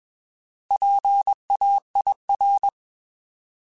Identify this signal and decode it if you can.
Morse code, heard as a keyed tone, about 21 wpm, 800 Hz. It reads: PAIR
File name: Alphabetical / Alphabetical mp3s / Ps @20